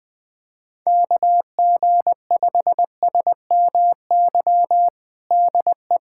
Morse code, heard as a keyed tone, about 20 wpm, 700 Hz. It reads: KG5SMY DE
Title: USA Callsign DE Callsign - 6 Characters - 20wpm